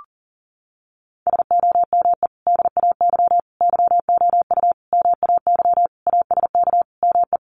Morse code, heard as a keyed tone, about 40 wpm, 700 Hz. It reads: SOME DAY YOU MAY ASK ME
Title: Sentences from Top 200 Words 40wpm